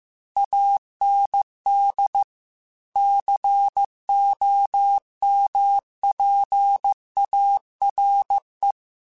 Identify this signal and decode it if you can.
Morse code, heard as a keyed tone, about 15 wpm, 800 Hz. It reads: AND COMPARE
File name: Technical Subjects / Technical Subjects-mp3s / OR Gate @13